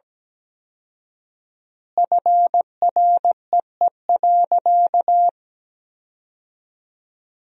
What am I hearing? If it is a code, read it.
Morse code, heard as a keyed tone, about 17 wpm, 700 Hz. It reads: FREE.
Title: 4x Word-Spacing - Sentences from Top 500 Words - 17wpm